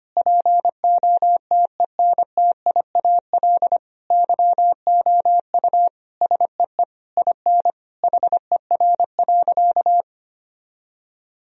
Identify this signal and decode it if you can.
Morse code, heard as a keyed tone, about 25 wpm, 700 Hz. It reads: POTENTIAL YOU SEE IN HER.
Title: Sentences from Top 1000 Words - Review Part 4 - 25wpm